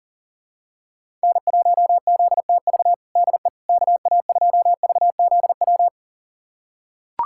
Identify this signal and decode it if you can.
Morse code, heard as a keyed tone, about 40 wpm, 700 Hz. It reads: N1ZTV DE KA2VZW
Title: USA Callsign DE Callsign 40wpm